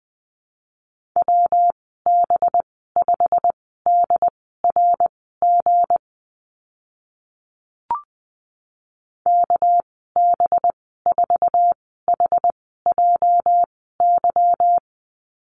Morse code, heard as a keyed tone, about 20 wpm, spaced as for 14 wpm, 700 Hz. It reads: WB5DRG KB4HJY
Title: 231 Lesson 37g - Letter x - Call Signs 20wpm